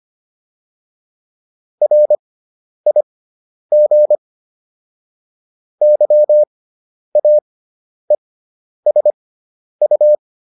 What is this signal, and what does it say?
Morse code, heard as a keyed tone, about 25 wpm, spaced as for 10 wpm, 600 Hz. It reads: RIG YAESU